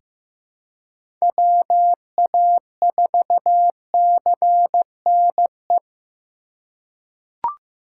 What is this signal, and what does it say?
Morse code, heard as a keyed tone, about 15 wpm, 700 Hz. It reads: WA4CNE
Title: USA Call Signs - 3x Character Spacing - 15wpm